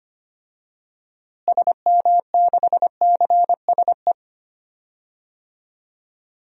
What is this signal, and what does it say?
Morse code, heard as a keyed tone, about 25 wpm, 700 Hz. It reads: SM6CSE
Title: DXCC-100 Call Signs - 6 Characters - Repeated 2x - 25wpm